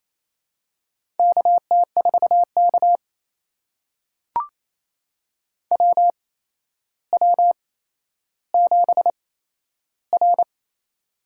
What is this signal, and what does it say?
Morse code, heard as a keyed tone, about 28 wpm, spaced as for 12 wpm, 700 Hz. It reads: KT4K WW7R
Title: USA Call Signs - 4 Characters - 8x Character Spacing - 28wpm